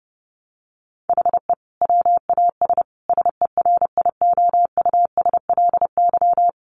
Morse code, heard as a keyed tone, about 28 wpm, 700 Hz. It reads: HE WAS SERIOUSLY
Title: Sentences from Top 1200 Words 30wpm